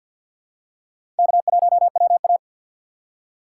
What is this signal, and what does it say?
Morse code, heard as a keyed tone, about 52 wpm, 690 Hz. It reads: K1WA
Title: USA Call Signs - 3x Character Spacing - 50wpm